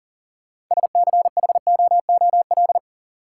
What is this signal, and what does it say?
Morse code, heard as a keyed tone, about 37 wpm, 700 Hz. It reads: SCHOOL
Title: Sets of 2 Words - New Word Drill - 40wpm